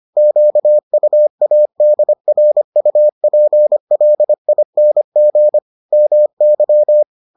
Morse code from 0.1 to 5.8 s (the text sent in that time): QUADRUPLING